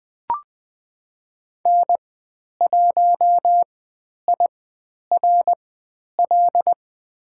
Morse code, heard as a keyed tone, about 20 wpm, spaced as for 10 wpm, 700 Hz. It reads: N1IRL
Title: 095 Lesson 16g - Letter c - Call Signs 20wpm